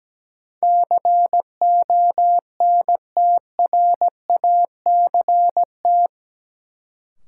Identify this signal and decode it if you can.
Morse code, heard as a keyed tone, about 17 wpm, 700 Hz. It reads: CONTRACT
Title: Sentences from Top 1400 Words - New Word Drill - 17wpm